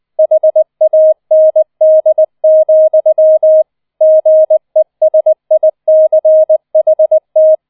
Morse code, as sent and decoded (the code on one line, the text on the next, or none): .... .- -. -.. --..-- / --. ... .. -.-. .... -
HAND, GESICHT